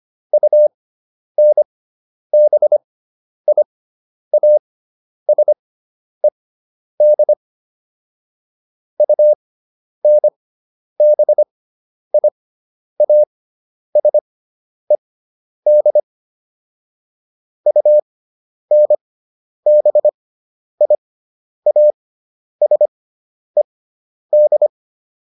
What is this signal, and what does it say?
Morse code, heard as a keyed tone, about 25 wpm, spaced as for 10 wpm, 600 Hz. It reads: UNBIASED UNBIASED UNBIASED